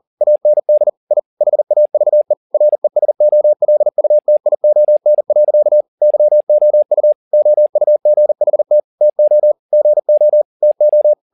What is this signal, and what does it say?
Morse code, heard as a keyed tone, about 40 wpm, 600 Hz. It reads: AND I HAVE RESOLUTION. YOU OUGHT TO GO TO